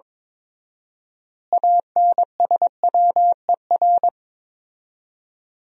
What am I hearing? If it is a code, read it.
Morse code, heard as a keyed tone, about 21 wpm, 700 Hz. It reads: ANSWER
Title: 4x Word-Spacing - Sentences from Top 300 Words - 22wpm